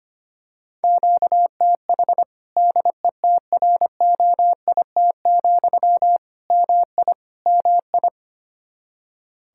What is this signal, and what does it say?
Morse code, heard as a keyed tone, about 25 wpm, 700 Hz. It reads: QTH DETROIT, MI MI
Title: Practice QSO - Exchange 1 - Beginning - 25wpm